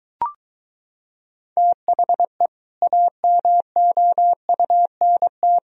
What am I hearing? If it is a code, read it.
Morse code, heard as a keyed tone, about 22 wpm, 700 Hz. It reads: THE AMOUNT